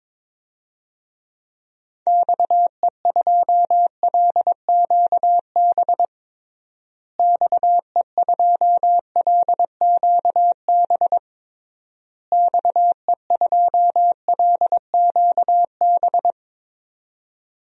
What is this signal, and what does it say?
Morse code, heard as a keyed tone, about 22 wpm, 700 Hz. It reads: XE2LQB XE2LQB XE2LQB
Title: DXCC-100 Call Signs - 6 Characters - Repeated 3x - 22wpm